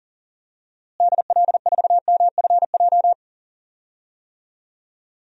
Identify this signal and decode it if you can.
Morse code, heard as a keyed tone, about 37 wpm, 700 Hz. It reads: DL4MFJ
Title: DXCC-100 Call Signs - 6 Characters - Repeated 3x - 40wpm